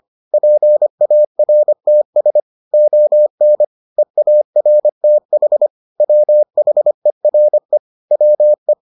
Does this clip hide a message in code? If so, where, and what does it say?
Yes: Morse code from 0.3 to 8.7 s, heard as a keyed tone, about 25 wpm, 600 Hz. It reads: PARTS ON EARTH WHERE WE